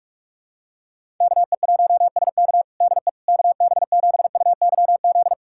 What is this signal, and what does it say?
Morse code, heard as a keyed tone, about 43 wpm, 700 Hz. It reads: KE1SK DE KB7UYZ
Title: USA Callsign DE Callsign 45wpm